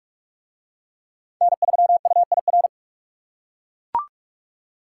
Morse code, heard as a keyed tone, about 42 wpm, 700 Hz. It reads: N3UIR